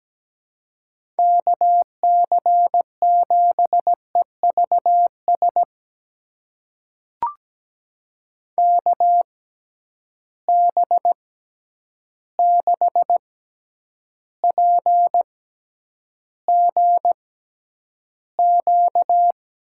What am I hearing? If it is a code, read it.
Morse code, heard as a keyed tone, about 17 wpm, spaced as for 9 wpm, 700 Hz. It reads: KC7EVS KB6PGQ
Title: USA Call Signs - 6 Characters - 6x Character Spacing - 17wpm